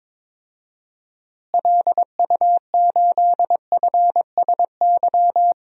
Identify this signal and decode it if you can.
Morse code, heard as a keyed tone, about 22 wpm, 700 Hz. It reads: LU8FSY
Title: DXCC-100 Call Signs - 6 Characters - Repeated 2x - 22wpm